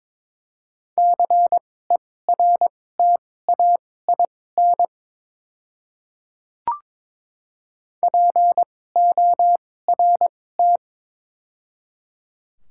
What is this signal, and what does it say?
Morse code, heard as a keyed tone, about 22 wpm, spaced as for 16 wpm, 700 Hz. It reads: CERTAIN PORT
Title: 2x Character Spacing - Top 500 Words - 22wpm